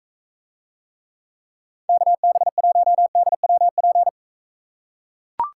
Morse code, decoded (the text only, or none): KB1DWP